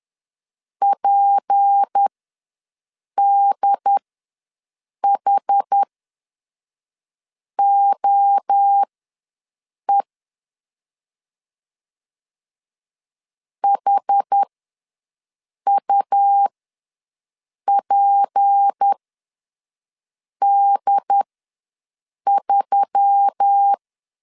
Morse code, heard as a keyed tone, about 10 wpm, spaced as for 5 wpm, 800 Hz. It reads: PDHOE HUPD3
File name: LEZIONI / 6 PKX0 / 6 P ws 10wpm 3751